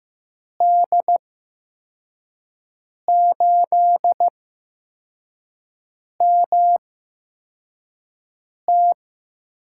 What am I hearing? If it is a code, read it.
Morse code, heard as a keyed tone, about 15 wpm, spaced as for 4 wpm, 700 Hz. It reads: D8MT
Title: USA Call Signs - 6 Characters - 8x Character Spacing - 15wpm